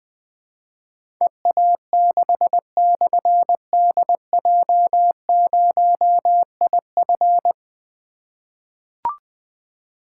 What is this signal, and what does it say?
Morse code, heard as a keyed tone, about 20 wpm, 700 Hz. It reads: EA6/DJ0IF
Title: Difficult Call Signs 20wpm